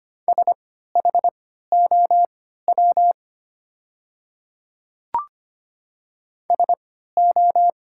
Morse code, heard as a keyed tone, about 25 wpm, spaced as for 14 wpm, 700 Hz. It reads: SHOW SO
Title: 3x Character Spacing - Top 200 Words - 25wpm